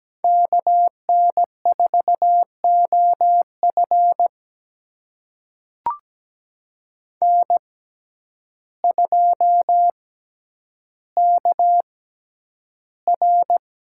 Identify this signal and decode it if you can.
Morse code, heard as a keyed tone, about 17 wpm, spaced as for 9 wpm, 700 Hz. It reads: KN4OF N2KR
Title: USA Call Signs - 5 Characters - 6x Character Spacing - 17wpm